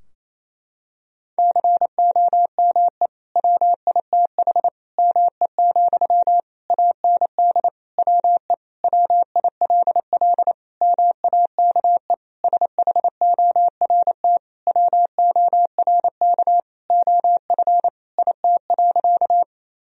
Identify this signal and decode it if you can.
Morse code, heard as a keyed tone, about 28 wpm, 700 Hz. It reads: COME WITH ME, AND WE WILL MAKE SHORT WORK OF IT.
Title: Sentences from Top 1000 Words - Review Part 2 - 28wpm